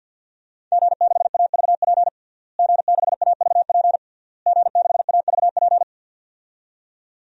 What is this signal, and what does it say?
Morse code, heard as a keyed tone, about 53 wpm, 700 Hz. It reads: G6AVP G6AVP G6AVP
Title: DXCC-100 Call Signs - 5 Characters - Repeated 3x - 50wpm